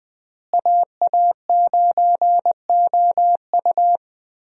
Morse code, heard as a keyed tone, about 21 wpm, 700 Hz. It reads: AA9OU
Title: USA Call Signs - 5 Characters - Repeated 3x - 20wpm